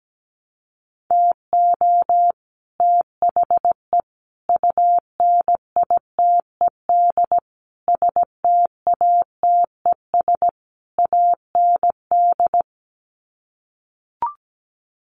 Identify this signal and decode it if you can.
Morse code, heard as a keyed tone, about 17 wpm, 700 Hz. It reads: TO THE UNITED STATES AND
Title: Sets of 5 Words Encore 17wpm